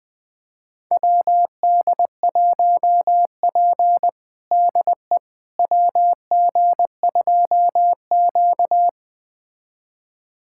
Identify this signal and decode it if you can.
Morse code, heard as a keyed tone, about 20 wpm, 700 Hz. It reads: WD1P DE WG2Q
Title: USA Callsign DE Callsign - 4 Characters - 20wpm